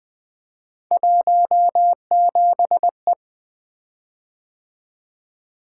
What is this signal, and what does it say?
Morse code, heard as a keyed tone, about 20 wpm, 700 Hz. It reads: 17E
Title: Three Letter-Number Combinations - 20wpm